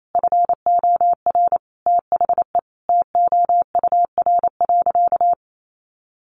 Morse code, heard as a keyed tone, about 28 wpm, 700 Hz. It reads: FOR THE TOUR.
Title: Sentences from Top 1400 Words 28wpm